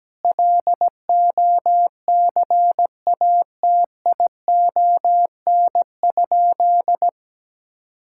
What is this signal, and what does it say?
Morse code, heard as a keyed tone, about 17 wpm, 700 Hz. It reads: LOCATION?
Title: Sentences from Top 1000 Words - Review Part 6 - 17wpm